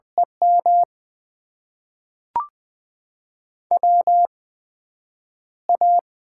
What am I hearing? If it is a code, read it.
Morse code, heard as a keyed tone, about 21 wpm, spaced as for 9 wpm, 700 Hz. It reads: EM WA
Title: USA Call Signs - 8x Character Spacing - 20wpm